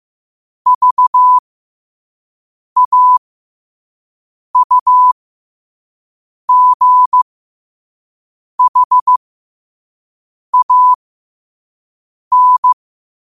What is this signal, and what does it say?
Morse code, heard as a keyed tone, about 15 wpm, spaced as for 5 wpm, 1000 Hz. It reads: VAUGHAN
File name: qotdfiles / 5wpm